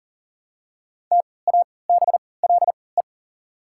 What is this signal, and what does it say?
Morse code, heard as a keyed tone, about 39 wpm, spaced as for 22 wpm, 700 Hz. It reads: TABLE
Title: 3x Character Spacing - Top 500 Words - 40wpm